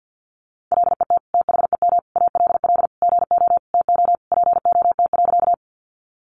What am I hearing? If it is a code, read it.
Morse code, heard as a keyed tone, about 50 wpm, 700 Hz. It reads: LET THEM ALL GO TO POT.
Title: Sentences from Top 1000 Words 50wpm